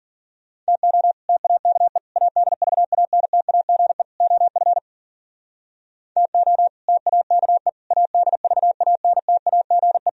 Morse code, heard as a keyed tone, about 54 wpm, 700 Hz. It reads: TO TAKE ADVANTAGE OF TO TAKE ADVANTAGE